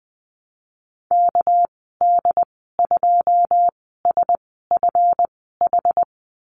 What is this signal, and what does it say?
Morse code, heard as a keyed tone, about 19 wpm, spaced as for 14 wpm, 700 Hz. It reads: KD2SFH
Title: 207 Lesson 33g - Letter j - Call Signs 20wpm